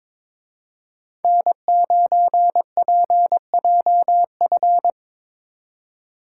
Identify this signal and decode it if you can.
Morse code, heard as a keyed tone, about 26 wpm, 700 Hz. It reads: N9PJF
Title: Speed Racing - USA 1x3 Call Signs 22wpm